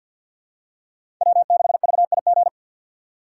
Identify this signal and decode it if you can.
Morse code, heard as a keyed tone, about 50 wpm, 700 Hz. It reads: W6VIG
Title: USA Call Signs - 5 Characters - 6x Character Spacing - 50wpm